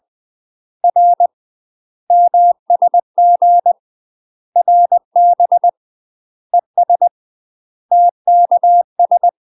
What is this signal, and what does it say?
Morse code, heard as a keyed tone, about 20 wpm, 700 Hz. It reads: R MSG RB ES TKS
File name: UK MASTER QSOs TRAINING / 20WPM / UK MASTER QSOs 0023